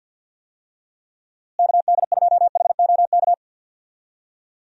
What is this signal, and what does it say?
Morse code, heard as a keyed tone, about 50 wpm, 700 Hz. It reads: KD2HOK
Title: USA Call Signs - 6 Characters - Repeated 2x - 50wpm